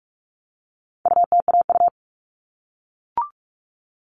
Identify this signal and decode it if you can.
Morse code, heard as a keyed tone, about 44 wpm, 700 Hz. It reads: UTAU
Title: All Cut Numbers - 4-Digit 45wpm